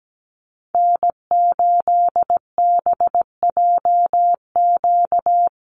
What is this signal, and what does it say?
Morse code, heard as a keyed tone, about 17 wpm, 700 Hz. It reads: N8BJQ
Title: Call Signs - 17wpm